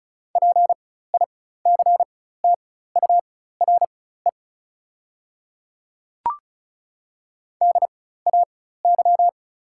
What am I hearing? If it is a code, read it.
Morse code, heard as a keyed tone, about 36 wpm, spaced as for 16 wpm, 700 Hz. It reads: PICTURE DAY
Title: 4x Character Spacing - Top 200 Words - 35wpm